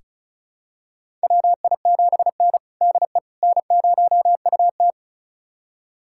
Morse code, heard as a keyed tone, about 36 wpm, 700 Hz. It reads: WI7N DE N0UT
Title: USA Callsign DE Callsign - 4 Characters - 35wpm